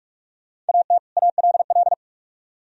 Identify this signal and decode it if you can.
Morse code, heard as a keyed tone, about 42 wpm, 700 Hz. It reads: AT ALL